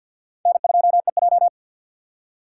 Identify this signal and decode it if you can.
Morse code, heard as a keyed tone, about 60 wpm, 700 Hz. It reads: N2EJ
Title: Speed Racing - USA 1x2 Call Signs 50wpm